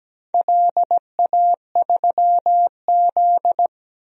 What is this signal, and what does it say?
Morse code, heard as a keyed tone, about 17 wpm, 700 Hz. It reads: LA3Z
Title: DXCC-100 Call Signs - 4 Characters - 17wpm